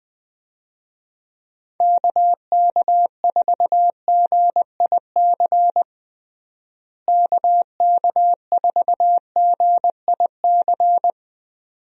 Morse code, heard as a keyed tone, about 20 wpm, 700 Hz. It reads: KK4GIC KK4GIC
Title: USA Call Signs - 6 Characters - Repeated 2x - 20wpm